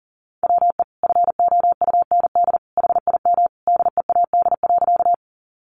Morse code, heard as a keyed tone, about 40 wpm, 700 Hz. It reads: WE FOUND HIM DEAD.
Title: Sentences from Top 1500 Words 40wpm